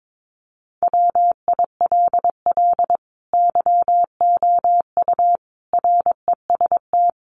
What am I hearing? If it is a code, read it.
Morse code, heard as a keyed tone, about 22 wpm, 700 Hz. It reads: WILL YOU REST
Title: Sentences from Top 500 Words 22wpm